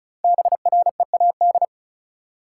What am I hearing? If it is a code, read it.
Morse code, heard as a keyed tone, about 36 wpm, 700 Hz. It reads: BREAD
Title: Sentences from Top 1600 Words - New Word Drill - 35wpm